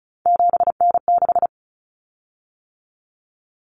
Morse code, heard as a keyed tone, about 34 wpm, 700 Hz. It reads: 7N6
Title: Common Cut Numbers - 3-Digit 35wpm